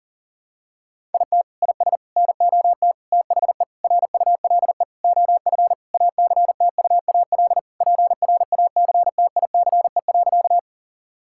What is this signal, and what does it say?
Morse code, heard as a keyed tone, about 40 wpm, 700 Hz. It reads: IT IS NOT THE RULE OF ACTUAL PRACTICE.
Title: Sentences from Top 1000 Words - Review Part 5 - 40wpm